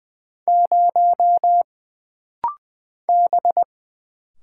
Morse code, heard as a keyed tone, about 20 wpm, 700 Hz. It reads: 0 B
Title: Single Letter-Number Alt - Rapid-Fire 20wpm